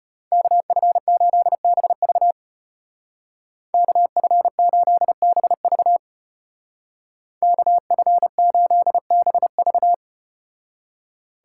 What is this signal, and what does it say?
Morse code, heard as a keyed tone, about 34 wpm, 700 Hz. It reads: KF8BV KF8BV KF8BV